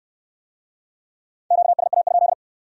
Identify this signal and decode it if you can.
Morse code, heard as a keyed tone, about 60 wpm, 700 Hz. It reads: QSA?